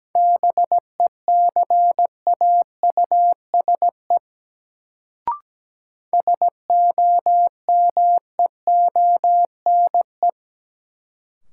Morse code, heard as a keyed tone, about 17 wpm, 700 Hz. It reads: BECAUSE SOMEONE
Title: Sentences from Top 600 Words - New Word Drill - 17wpm